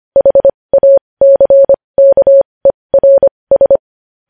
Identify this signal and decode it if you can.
Morse code, heard as a keyed tone, about 25 wpm, spaced as for 20 wpm, 550 Hz. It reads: HACKERS
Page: news-headlines-in-morse-code-at-20-wpm